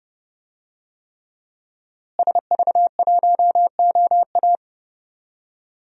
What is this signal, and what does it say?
Morse code, heard as a keyed tone, about 30 wpm, 700 Hz. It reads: SV1OA
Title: DXCC-100 Call Signs - 5 Characters - Repeated 3x - 30wpm